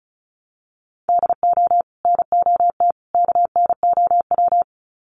Morse code, heard as a keyed tone, about 34 wpm, 700 Hz. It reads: DO NOT KNOW
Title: Sets of 3 Words 35wpm